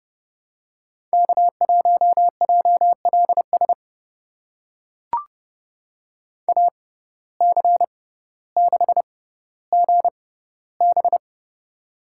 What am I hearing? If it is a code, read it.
Morse code, heard as a keyed tone, about 32 wpm, spaced as for 16 wpm, 700 Hz. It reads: K1JLS AC6GB
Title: USA Call Signs - 5 Characters - 6x Character Spacing - 30wpm